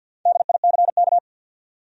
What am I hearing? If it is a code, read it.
Morse code, heard as a keyed tone, about 50 wpm, 700 Hz. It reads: DICK
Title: USA Common Names - Max 5 Letters - 50wpm